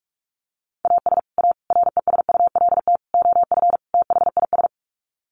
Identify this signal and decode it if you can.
Morse code, heard as a keyed tone, about 44 wpm, 700 Hz. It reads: AS A RESULT OF THIS
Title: Sets of 5 Words 45wpm